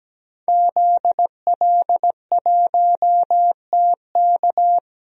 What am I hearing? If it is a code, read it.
Morse code, heard as a keyed tone, about 17 wpm, 700 Hz. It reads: ZL1TK
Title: DXCC-100 Call Signs - 5 Characters - 17wpm